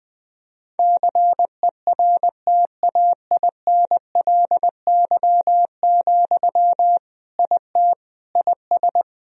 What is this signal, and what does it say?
Morse code, heard as a keyed tone, about 20 wpm, 700 Hz. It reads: CERTAINLY, IT IS
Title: Sentences from Top 1000 Words - Review Part 2 - 20wpm